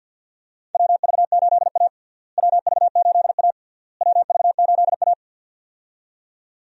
Morse code, heard as a keyed tone, about 51 wpm, 700 Hz. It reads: WV8A WV8A WV8A
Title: USA Call Signs - 4 Characters - Repeated 3x - 50wpm